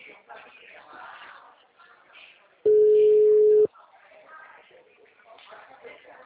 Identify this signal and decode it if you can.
Ringback tone (Europe, 425 Hz), with speech.